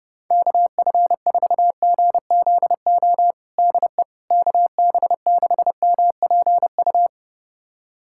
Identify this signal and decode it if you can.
Morse code, heard as a keyed tone, about 32 wpm, 700 Hz. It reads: KF4GZO DE KB6MPU
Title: USA Callsign DE Callsign - 6 Characters - 30wpm